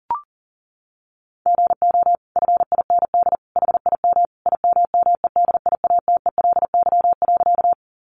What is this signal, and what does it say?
Morse code, heard as a keyed tone, about 40 wpm, 700 Hz. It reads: GO FIND HIM IMMEDIATELY.
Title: Sentences from Top 1100 Words 40wpm